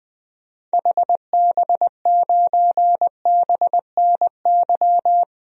Morse code, heard as a keyed tone, about 21 wpm, 700 Hz. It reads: HB9BNY
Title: DXCC-100 Call Signs - 6 Characters - 20wpm